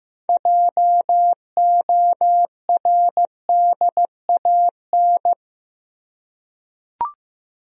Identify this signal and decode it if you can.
Morse code, heard as a keyed tone, about 15 wpm, 700 Hz. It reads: JORDAN
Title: USA Common Names - 15wpm